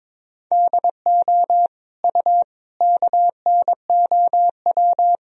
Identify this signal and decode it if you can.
Morse code, heard as a keyed tone, about 22 wpm, 700 Hz. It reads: DO U KNOW
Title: Sentences from Top 100 Words and Abbreviations 22wpm